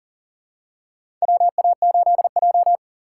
Morse code, heard as a keyed tone, about 40 wpm, 700 Hz. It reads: WA8J
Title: USA Call Signs - 6x Character Spacing - 40wpm